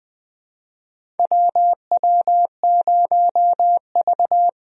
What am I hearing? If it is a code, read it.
Morse code, heard as a keyed tone, about 20 wpm, 700 Hz. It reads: WW0V
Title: USA Call Signs - 4 Characters - Repeated 3x - 20wpm